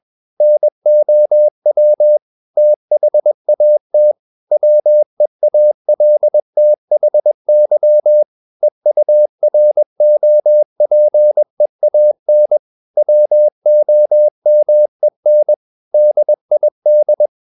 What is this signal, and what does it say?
Morse code, heard as a keyed tone, about 21 wpm, 600 Hz. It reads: NOW THAT WEALTHY EUROPEAN WOMEN DID
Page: CW med Gnister